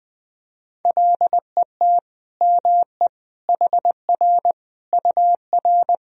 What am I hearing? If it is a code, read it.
Morse code, heard as a keyed tone, about 20 wpm, 700 Hz. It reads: LET ME HR UR
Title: Sentences from Top 300 Words and Abbreviations 20wpm